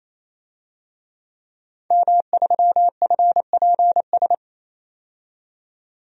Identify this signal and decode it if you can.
Morse code, heard as a keyed tone, about 28 wpm, 700 Hz. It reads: M3FPS
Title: DXCC-100 Call Signs - 5 Characters - 28wpm